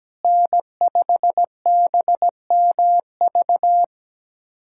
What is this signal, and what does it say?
Morse code, heard as a keyed tone, about 17 wpm, 700 Hz. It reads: N5BMV
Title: USA Callsign DE Callsign - 5 Characters - 17wpm